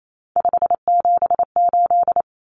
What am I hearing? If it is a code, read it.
Morse code, heard as a keyed tone, about 28 wpm, 700 Hz. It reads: 578
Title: Numbers - 3-Digit 28wpm